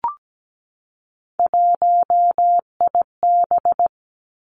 Morse code, heard as a keyed tone, about 17 wpm, 700 Hz. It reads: 1IB
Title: Three Challenging Letter-Number Combinations 17wpm